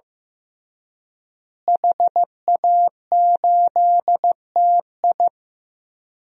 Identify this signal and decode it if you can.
Morse code, heard as a keyed tone, about 15 wpm, 700 Hz. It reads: HA8TI
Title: DXCC-100 Call Signs - 5 Characters - Repeated 3x - 15wpm